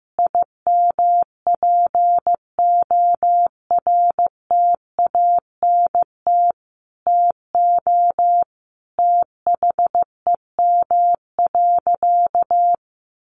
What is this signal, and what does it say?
Morse code, heard as a keyed tone, about 15 wpm, 700 Hz. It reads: IMPORTANT TO THEM.
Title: Sentences from Top 1400 Words 15wpm